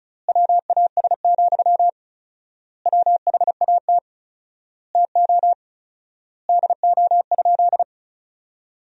Morse code, heard as a keyed tone, about 34 wpm, 700 Hz. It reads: WAS, WHAT TO DO?
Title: 4x Word-Spacing - Sentences from Top 400 Words - 35wpm